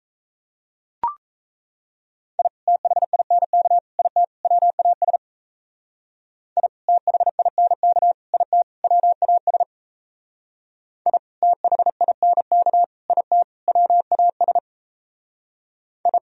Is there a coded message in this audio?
Yes: Morse code from 2.4 to 16.2 s, heard as a keyed tone, about 36 wpm, 700 Hz. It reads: I THINK IT WAS I THINK IT WAS I THINK IT WAS I